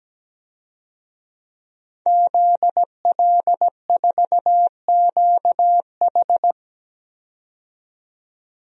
Morse code, heard as a keyed tone, about 17 wpm, 700 Hz. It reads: ZL4QH
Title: DXCC-100 Call Signs - 5 Characters - 17wpm